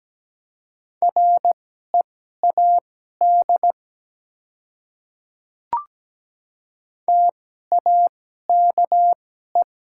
Morse code, heard as a keyed tone, about 17 wpm, spaced as for 12 wpm, 700 Hz. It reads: READ TAKE
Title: 2x Character Spacing - Top 200 Words - 17wpm